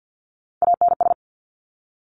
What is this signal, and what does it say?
Morse code, heard as a keyed tone, about 50 wpm, 700 Hz. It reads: ANS
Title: Top Three Letter Combinations in Words 50wpm